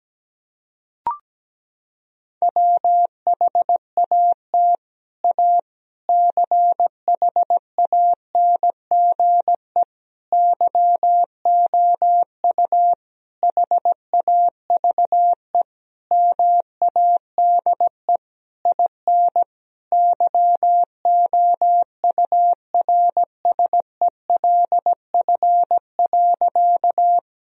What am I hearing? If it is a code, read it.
Morse code, heard as a keyed tone, about 17 wpm, 700 Hz. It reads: WHAT A CHANGE YOU HAVE MADE IN YOURSELF.
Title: Sentences from Top 1000 Words - Review Part 1 - 17wpm